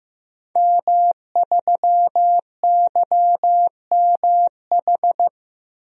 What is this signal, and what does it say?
Morse code, heard as a keyed tone, about 15 wpm, 700 Hz. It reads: M3YMH
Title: DXCC-100 Call Signs - 5 Characters - Repeated 2x - 15wpm